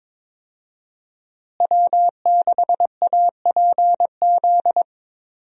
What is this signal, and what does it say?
Morse code, heard as a keyed tone, about 22 wpm, 700 Hz. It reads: W6APZ
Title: USA Call Signs - 5 Characters - Repeated 3x - 22wpm